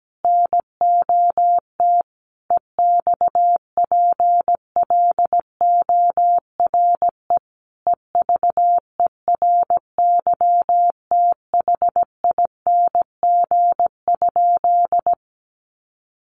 Morse code, heard as a keyed tone, about 17 wpm, 700 Hz. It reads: NOT EXPLORE EVERYTHING?
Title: Sentences from Top 1100 Words 17wpm